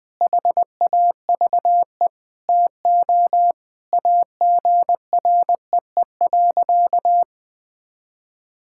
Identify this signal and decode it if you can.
Morse code, heard as a keyed tone, about 21 wpm, 700 Hz. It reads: HAVE TO AGREE.
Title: Sentences with 4-Word Phrases - Spoken Beginning - 20wpm